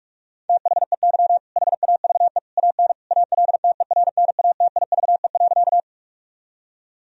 Morse code, heard as a keyed tone, about 46 wpm, 700 Hz. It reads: THEY HAVE AN ALTERNATIVE.
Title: Sentences from Top 1000 Words - Review Part 4 - 45wpm